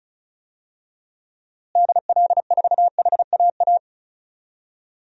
Morse code, heard as a keyed tone, about 36 wpm, 700 Hz. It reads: DL4HAA
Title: DXCC-100 Call Signs - 6 Characters - Repeated 3x - 35wpm